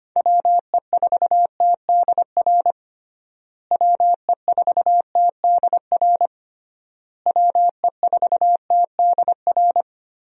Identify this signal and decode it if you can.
Morse code, heard as a keyed tone, about 25 wpm, 700 Hz. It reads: WE4TDR WE4TDR WE4TDR
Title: USA Call Signs - 6 Characters - Repeated 3x - 25wpm